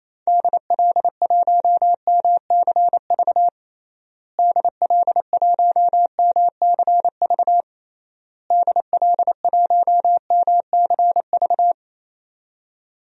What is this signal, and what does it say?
Morse code, heard as a keyed tone, about 28 wpm, 700 Hz. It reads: DL1MCV DL1MCV DL1MCV
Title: DXCC-100 Call Signs - 6 Characters - Repeated 3x - 28wpm